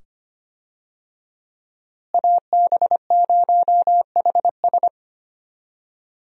Morse code, heard as a keyed tone, about 25 wpm, 700 Hz. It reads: AB0HS
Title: USA Call Signs - 5 Characters - Repeated 2x - 25wpm